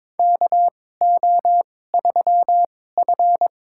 Morse code, heard as a keyed tone, about 22 wpm, spaced as for 16 wpm, 700 Hz. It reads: KO3F
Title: USA Call Signs - 4 Characters - 2x Character Spacing - 22wpm